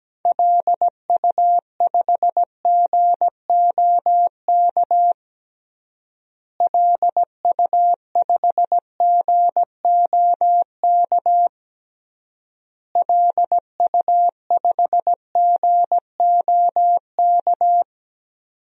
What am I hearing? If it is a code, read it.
Morse code, heard as a keyed tone, about 17 wpm, 700 Hz. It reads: LU5GOK LU5GOK LU5GOK
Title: DXCC-100 Call Signs - 6 Characters - Repeated 3x - 17wpm